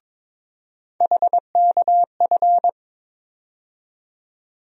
Morse code, heard as a keyed tone, about 22 wpm, 700 Hz. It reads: HKF
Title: USA Call Sign Postfixes - 3 Letters 22wpm